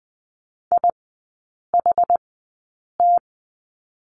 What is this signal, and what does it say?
Morse code, heard as a keyed tone, about 20 wpm, spaced as for 8 wpm, 700 Hz. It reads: IHT
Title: 060 Lesson 11e - Letter h - Three Characters 20wpm